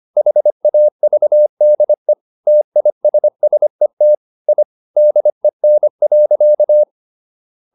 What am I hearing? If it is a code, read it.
Morse code, heard as a keyed tone, about 25 wpm, 600 Hz. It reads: HAVDE TISSET I DEN.
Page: Ferie 25 wpm